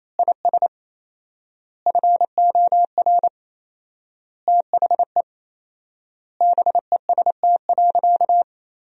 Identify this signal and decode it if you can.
Morse code, heard as a keyed tone, about 28 wpm, 700 Hz. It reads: IS FOR THE BEST.
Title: 4x Word-Spacing - Sentences from Top 500 Words - 28wpm